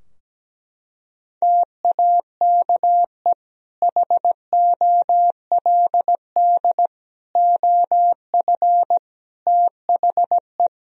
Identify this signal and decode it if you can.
Morse code, heard as a keyed tone, about 17 wpm, 700 Hz. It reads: TAKE HOLD OF THE
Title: Sentences from Top 1000 Words - Review Part 3 - 17wpm